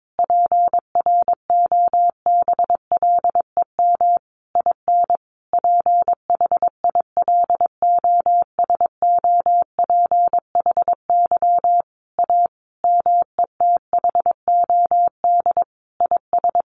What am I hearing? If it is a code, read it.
Morse code, heard as a keyed tone, about 22 wpm, 700 Hz. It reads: PROBLEM IN PHILOSOPHY A METHOD IS
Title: Sentences from Top 1400 Words 22wpm